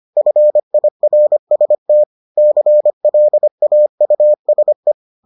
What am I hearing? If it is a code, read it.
Morse code, heard as a keyed tone, about 25 wpm, 600 Hz. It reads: FIRST CLAUSE